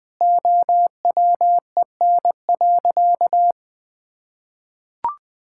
Morse code, heard as a keyed tone, about 21 wpm, 700 Hz. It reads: OWEN.
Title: Pangram Challenge v2 - 20wpm